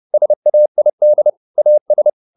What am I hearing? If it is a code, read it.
Morse code, heard as a keyed tone, about 30 wpm, 600 Hz. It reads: SAID AS
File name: the old man and the sea / the old man and the sea 1 30wpm